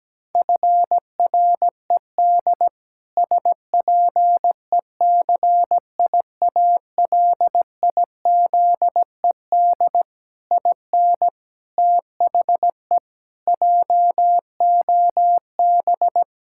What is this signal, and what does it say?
Morse code, heard as a keyed tone, about 17 wpm, 700 Hz. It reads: FRED SPECIALIZED IN THE JOB
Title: Pangram Challenge v2 - 17wpm